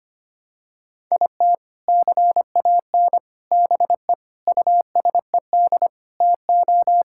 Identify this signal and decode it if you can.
Morse code, heard as a keyed tone, about 25 wpm, 700 Hz. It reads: IT CAN BE USED TO
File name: practice / 25wpm / Sentences with 4-Word Phrases - 25wpm